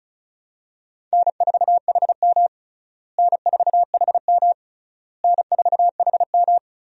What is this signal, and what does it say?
Morse code, heard as a keyed tone, about 34 wpm, 700 Hz. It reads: N4HM N4HM N4HM
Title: USA Call Signs - 4 Characters - Repeated 3x - 35wpm